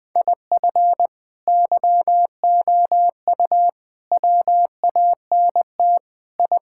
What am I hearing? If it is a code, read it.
Morse code, heard as a keyed tone, about 19 wpm, 700 Hz. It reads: IF YOU WANT I
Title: Sentences with 3-Word Phrases - Spoken Beginning - 20wpm